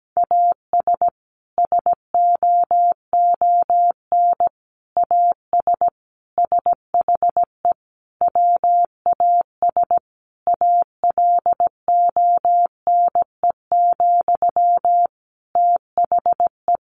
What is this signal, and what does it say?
Morse code, heard as a keyed tone, about 17 wpm, 700 Hz. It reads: AS SOON AS SHE WAS ALONE, THE
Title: Sentences from Top 1100 Words 17wpm